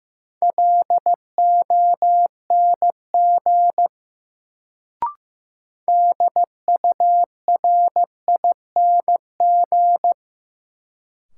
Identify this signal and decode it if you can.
Morse code, heard as a keyed tone, about 15 wpm, 700 Hz. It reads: LONG DURING
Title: Sets of 2 Words - New Word Drill - 15wpm